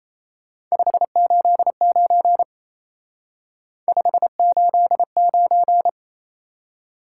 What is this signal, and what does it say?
Morse code, heard as a keyed tone, about 36 wpm, 700 Hz. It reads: 589 589